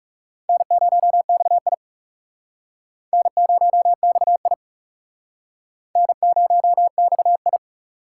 Morse code, heard as a keyed tone, about 35 wpm, 700 Hz. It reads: N0XI N0XI N0XI
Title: Speed Racing - USA 1x2 Call Signs 30wpm